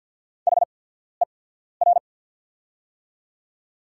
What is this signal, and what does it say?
Morse code, heard as a keyed tone, about 48 wpm, spaced as for 14 wpm, 690 Hz. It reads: HER
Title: 8x Character Spacing - Top 300 Words - 50wpm